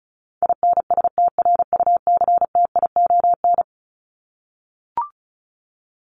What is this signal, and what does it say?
Morse code, heard as a keyed tone, about 34 wpm, 700 Hz. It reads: INSTRUCTION
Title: Top 1751-2000 Words 35wpm